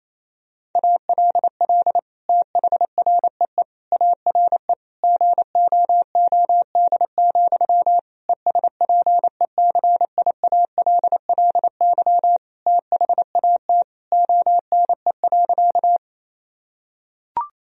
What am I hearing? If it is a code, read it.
Morse code, heard as a keyed tone, about 28 wpm, 700 Hz. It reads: ALL THREE ARE GOOD, ESPECIALLY THAT ONE.
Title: Sentences from Top 1000 Words - Review Part 4 - 28wpm